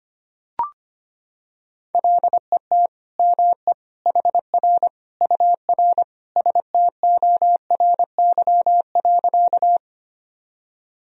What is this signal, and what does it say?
Morse code, heard as a keyed tone, about 25 wpm, 700 Hz. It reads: LET ME HR UR STORY.
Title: Sentences from Top 300 Words and Abbreviations 25wpm